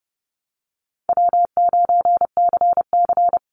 Morse code, heard as a keyed tone, about 30 wpm, 700 Hz. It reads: W9CC
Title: Call Signs Repeated 2x - 30wpm